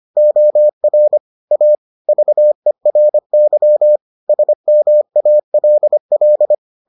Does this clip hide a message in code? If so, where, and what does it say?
Yes: Morse code from 0.2 to 6.5 s, heard as a keyed tone, about 25 wpm, 600 Hz. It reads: OR A VERY SMALL